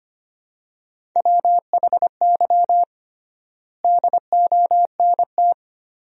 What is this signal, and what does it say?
Morse code, heard as a keyed tone, about 25 wpm, 700 Hz. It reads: WHY DONT
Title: 3x Word-Spacing - Sentences from Top 300 Words - 25wpm